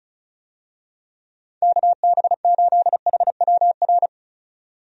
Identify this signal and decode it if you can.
Morse code, heard as a keyed tone, about 36 wpm, 700 Hz. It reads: KB8HWR
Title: USA Call Signs - 6 Characters - 35wpm